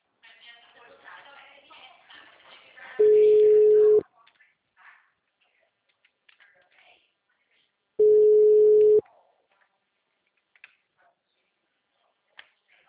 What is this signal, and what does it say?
Ringback tone (Europe, 425 Hz), with speech.
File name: out- -2002- - - .50064